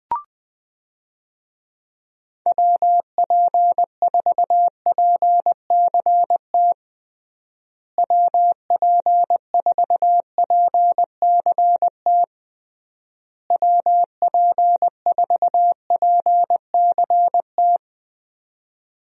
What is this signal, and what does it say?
Morse code, heard as a keyed tone, about 20 wpm, 700 Hz. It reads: WP4PCT WP4PCT WP4PCT
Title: USA Call Signs - 6 Characters - Repeated 3x - 20wpm